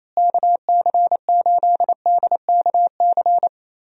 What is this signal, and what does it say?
Morse code, heard as a keyed tone, about 27 wpm, 700 Hz. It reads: KC8DKC
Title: USA Call Signs - 6 Characters - Repeated 3x - 28wpm